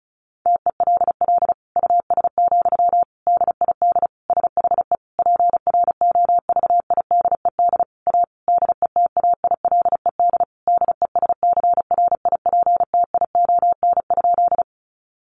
Morse code, heard as a keyed tone, about 34 wpm, 700 Hz. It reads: TELL US, DID SHE PROVIDED A DETAILED DESCRIPTION?
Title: Sentences from Top 1500 Words 35wpm